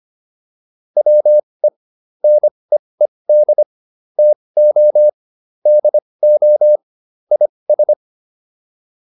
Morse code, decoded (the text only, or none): WE NEED TO DO IS